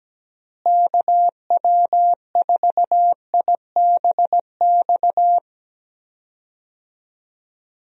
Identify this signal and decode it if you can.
Morse code, heard as a keyed tone, about 17 wpm, 700 Hz. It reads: KW4IBX
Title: USA Call Signs - 6 Characters - Repeated 3x - 17wpm